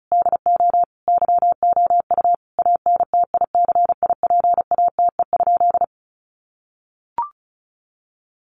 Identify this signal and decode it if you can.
Morse code, heard as a keyed tone, about 34 wpm, 700 Hz. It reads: DO YOU ANTICIPATE?
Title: Sentences from Top 1500 Words 35wpm